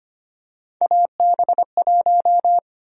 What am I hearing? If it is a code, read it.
Morse code, heard as a keyed tone, about 25 wpm, 700 Hz. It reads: AB1
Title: USA Call Sign Prefixes - 25wpm